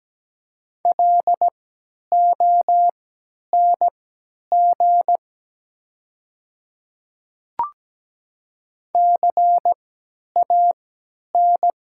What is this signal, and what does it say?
Morse code, heard as a keyed tone, about 17 wpm, spaced as for 10 wpm, 700 Hz. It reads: LONG CAN
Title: 3x Character Spacing - Top 100 Words - 17wpm